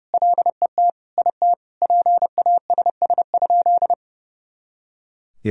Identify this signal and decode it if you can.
Morse code, heard as a keyed tone, about 30 wpm, 700 Hz. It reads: LET IT PASS?
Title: Sentences from Top 400 Words and Abbreviations 30wpm